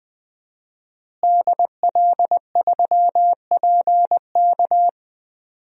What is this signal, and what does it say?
Morse code, heard as a keyed tone, about 21 wpm, 700 Hz. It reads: DL3PK
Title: DXCC-100 Call Signs - 5 Characters - Repeated 2x - 20wpm